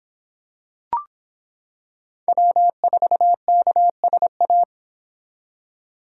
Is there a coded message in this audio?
Yes: Morse code from 2.3 to 4.6 s, heard as a keyed tone, about 25 wpm, 700 Hz. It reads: W4KSA